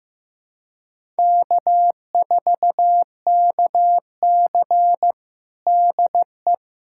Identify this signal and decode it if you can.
Morse code, heard as a keyed tone, about 16 wpm, 700 Hz. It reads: K4KC DE
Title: USA Callsign DE Callsign - 4 Characters - 15wpm